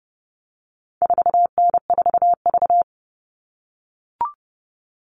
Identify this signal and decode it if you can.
Morse code, heard as a keyed tone, about 30 wpm, 700 Hz. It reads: 4N4V
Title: All Cut Numbers - 4-Digit 30wpm